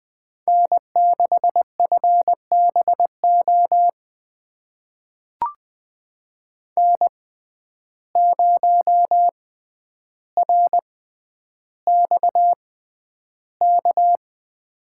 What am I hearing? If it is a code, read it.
Morse code, heard as a keyed tone, about 20 wpm, spaced as for 10 wpm, 700 Hz. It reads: N6FBO N0RXK
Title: USA Call Signs - 5 Characters - 6x Character Spacing - 20wpm